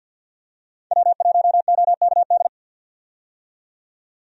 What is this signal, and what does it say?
Morse code, heard as a keyed tone, about 52 wpm, 700 Hz. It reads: W1OKD
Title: USA Call Signs - 5 Characters - 50wpm